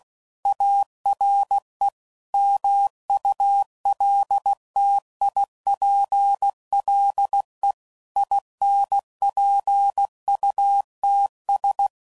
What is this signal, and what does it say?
Morse code, heard as a keyed tone, about 16 wpm, 800 Hz. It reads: ARE MULTIPLE INPUTS
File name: CW Code Groups @15WPM / Technical Subjects / AND Gate